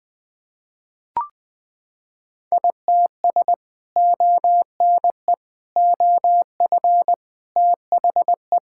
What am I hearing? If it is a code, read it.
Morse code, heard as a keyed tone, about 21 wpm, 700 Hz. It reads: ITS ONE OF THE
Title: Sentences with 3-Word Phrases - 20wpm